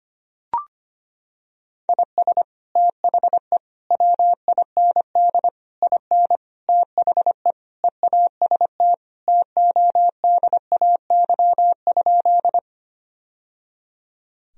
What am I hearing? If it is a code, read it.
Morse code, heard as a keyed tone, about 25 wpm, 700 Hz. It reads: IS THE WIND IN THE EAST TODAY?
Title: Sentences from Top 1000 Words - Review Part 3 - 25wpm